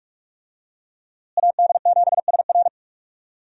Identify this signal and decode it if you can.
Morse code, heard as a keyed tone, about 41 wpm, 700 Hz. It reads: AD7SR
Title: USA Call Signs - 5 Characters - Repeated 3x - 45wpm